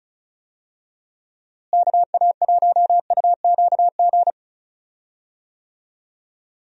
Morse code, heard as a keyed tone, about 36 wpm, 700 Hz. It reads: KA1UQG
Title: USA Call Signs - 6 Characters - Repeated 3x - 35wpm